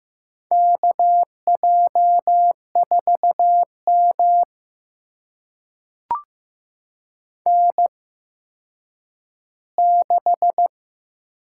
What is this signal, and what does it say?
Morse code, heard as a keyed tone, about 15 wpm, spaced as for 9 wpm, 700 Hz. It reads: KJ4M N6
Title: USA Call Signs - 4 Characters - 8x Character Spacing - 15wpm